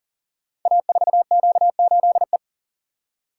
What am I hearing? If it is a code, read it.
Morse code, heard as a keyed tone, about 38 wpm, 690 Hz. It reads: A4Q8E
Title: Code Groups - 5 Letter:Numbers - 40wpm